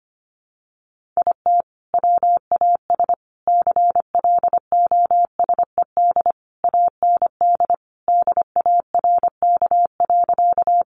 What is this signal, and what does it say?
Morse code, heard as a keyed tone, about 25 wpm, 700 Hz. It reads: IT WAS CLOSED AND DARK.
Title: Sentences from Top 1400 Words 25wpm